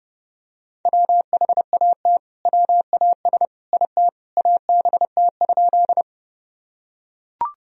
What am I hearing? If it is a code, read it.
Morse code, heard as a keyed tone, about 30 wpm, 700 Hz. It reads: WHAT WAS IT ABT?
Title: Sentences from Top 100 Words and Abbreviations 30wpm